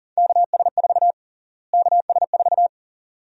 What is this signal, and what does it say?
Morse code, heard as a keyed tone, about 40 wpm, 700 Hz. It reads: KS4 KS4
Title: USA Call Sign Prefixes - Repeated 2x - 40wpm